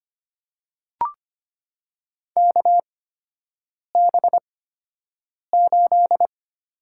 Morse code, heard as a keyed tone, about 25 wpm, spaced as for 7 wpm, 700 Hz. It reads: KB8
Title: USA Call Signs - 6 Characters - 8x Character Spacing - 25wpm